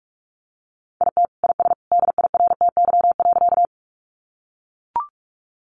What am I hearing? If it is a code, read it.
Morse code, heard as a keyed tone, about 44 wpm, 700 Hz. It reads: IT IS DIRTY.
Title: Additional Easy Phrases 45wpm